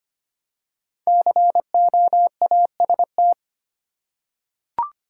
Morse code, heard as a keyed tone, about 25 wpm, 700 Hz. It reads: COAST
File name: 5 Letter Words - First 3 Letters Given - 25wpm